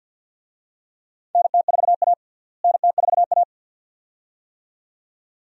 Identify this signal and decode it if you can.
Morse code, heard as a keyed tone, about 50 wpm, 700 Hz. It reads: NT4A NT4A
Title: USA Call Signs - 4 Characters - Repeated 2x - 50wpm